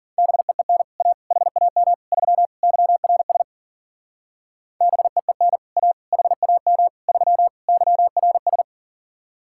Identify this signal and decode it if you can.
Morse code, heard as a keyed tone, about 48 wpm, 700 Hz. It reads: BEEN A HAM 3 YRS BEEN A HAM 3 YRS
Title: Speed-Racing Common QSO Phrases - 40wpm